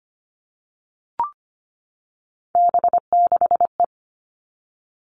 Morse code, heard as a keyed tone, about 25 wpm, 700 Hz. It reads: B6E
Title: All Cut Numbers - 3-Digit 25wpm